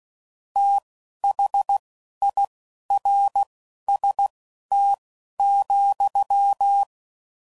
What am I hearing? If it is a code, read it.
Morse code, heard as a keyed tone, about 16 wpm, spaced as for 11 wpm, 800 Hz. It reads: THIRST,
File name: Prefixes / FLs & THs / Fast TH Prefixes